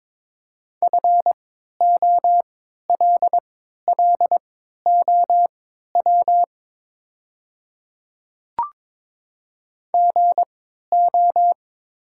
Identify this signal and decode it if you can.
Morse code, heard as a keyed tone, about 22 wpm, spaced as for 13 wpm, 700 Hz. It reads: FOLLOW GO
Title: 3x Character Spacing - Top 200 Words - 22wpm